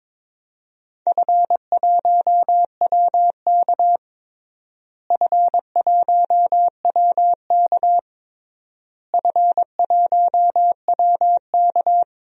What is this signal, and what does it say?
Morse code, heard as a keyed tone, about 22 wpm, 700 Hz. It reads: F1WK F1WK F1WK
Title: DXCC-100 Call Signs - 4 Characters - Repeated 3x - 22wpm